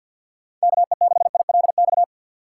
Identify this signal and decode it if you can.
Morse code, heard as a keyed tone, about 47 wpm, 700 Hz. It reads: KE6ILX
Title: USA Call Signs - 6 Characters - 6x Character Spacing - 50wpm